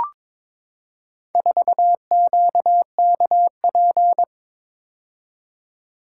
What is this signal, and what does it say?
Morse code, heard as a keyed tone, about 22 wpm, 700 Hz. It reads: 4QKP
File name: practice / 22wpm / Four Letter-Number Combinations - 22wpm